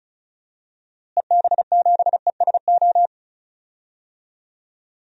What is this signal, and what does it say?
Morse code, heard as a keyed tone, about 36 wpm, 700 Hz. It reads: EB7ESO
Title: DXCC-100 Call Signs - 6 Characters - Repeated 2x - 35wpm